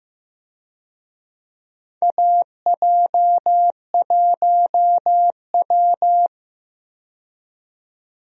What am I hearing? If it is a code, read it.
Morse code, heard as a keyed tone, about 15 wpm, 700 Hz. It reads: AJ1W
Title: USA Call Signs - 4 Characters - 15wpm